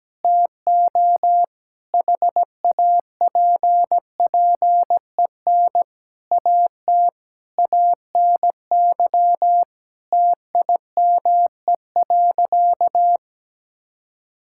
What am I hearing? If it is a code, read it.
Morse code, heard as a keyed tone, about 17 wpm, 700 Hz. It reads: TO HAPPEN AT ANY TIME.
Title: Sentences from Top 1000 Words - Review Part 6 - 17wpm